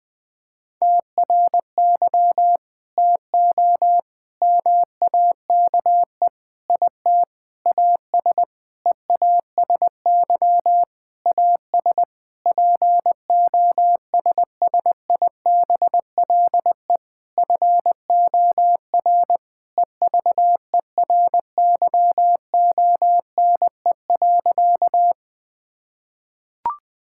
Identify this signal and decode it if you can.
Morse code, heard as a keyed tone, about 20 wpm, 700 Hz. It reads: TRY TO MAKE IT AS EASY AS POSSIBLE FOR EVERYONE.
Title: Sentences with 8-Word Phrases - 20wpm